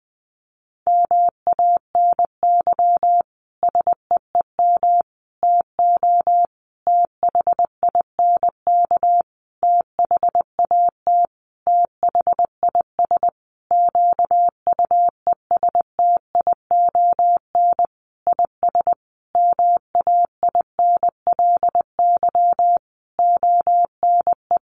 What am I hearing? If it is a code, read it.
Morse code, heard as a keyed tone, about 19 wpm, 700 Hz. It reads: MANY SEEM TO THINK THAT THIS QUESTION IS MAINLY ONE
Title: Sentences from Top 1400 Words 20wpm